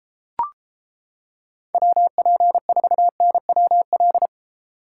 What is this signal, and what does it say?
Morse code, heard as a keyed tone, about 29 wpm, 700 Hz. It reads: WP4NWL